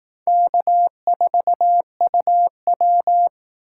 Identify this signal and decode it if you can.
Morse code, heard as a keyed tone, about 18 wpm, 700 Hz. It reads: K4UW
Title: Speed Racing - USA 1x2 Call Signs 15wpm